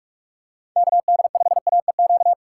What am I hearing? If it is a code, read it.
Morse code, heard as a keyed tone, about 43 wpm, 700 Hz. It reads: KD5AEQ